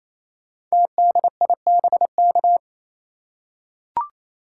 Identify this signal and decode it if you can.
Morse code, heard as a keyed tone, about 28 wpm, 700 Hz. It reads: TDIBK
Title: Code Groups - 5 Letters - 28wpm